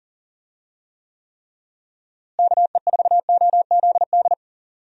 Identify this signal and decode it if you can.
Morse code, heard as a keyed tone, about 37 wpm, 700 Hz. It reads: KE4OZD
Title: USA Call Signs - 6 Characters - Repeated 3x - 40wpm